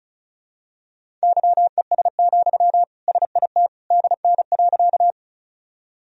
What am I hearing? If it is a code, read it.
Morse code, heard as a keyed tone, about 34 wpm, 700 Hz. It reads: YES, SIT DN.
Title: Sentences from Top 500 Words and Abbreviations 35wpm